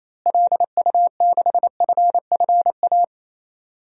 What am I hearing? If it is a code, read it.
Morse code, heard as a keyed tone, about 28 wpm, 700 Hz. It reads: LU6FFA
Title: DXCC-100 Call Signs - 6 Characters - Repeated 3x - 28wpm